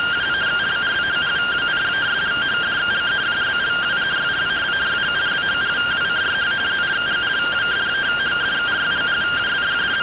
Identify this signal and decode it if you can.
Two-tone teleprinter signal, about 170 Hz shift.